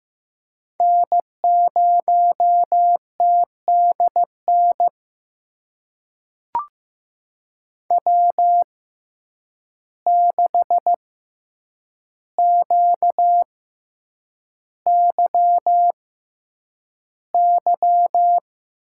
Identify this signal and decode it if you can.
Morse code, heard as a keyed tone, about 15 wpm, spaced as for 8 wpm, 700 Hz. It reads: N0TDN W6QYY
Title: USA Call Signs - 5 Characters - 6x Character Spacing - 15wpm